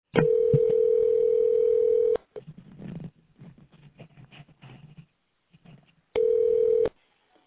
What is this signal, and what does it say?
Ringback tone (North America), 440 + 480 Hz.